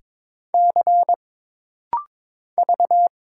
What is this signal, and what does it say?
Morse code, heard as a keyed tone, about 22 wpm, 700 Hz. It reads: C V
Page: This rapid-fire timing is geared to push your ICR (Instant Character Recognition) skills.